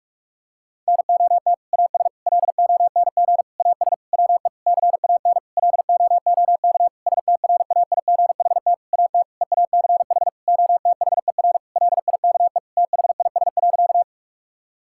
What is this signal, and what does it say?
Morse code, heard as a keyed tone, about 46 wpm, 700 Hz. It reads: NOT AS LONG AS WE CAN LOOK STRAIGHT AT EACH OTHER LIKE THIS.
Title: Sentences from Top 1000 Words - Review Part 6 - 45wpm